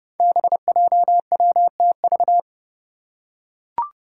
Morse code, heard as a keyed tone, about 30 wpm, 700 Hz. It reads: BJWTV
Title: Five Challenging Letter-Number Combinations 30wpm